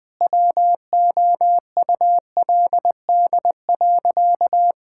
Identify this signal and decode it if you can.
Morse code, heard as a keyed tone, about 20 wpm, 700 Hz. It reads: WOULD.
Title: Sentences with 9-Word Phrases - 20wpm